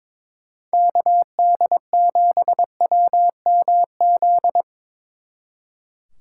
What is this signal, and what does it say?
Morse code, heard as a keyed tone, about 25 wpm, 700 Hz. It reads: KD7WMZ
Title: Speed Racing - USA 2x3 Call Signs 22wpm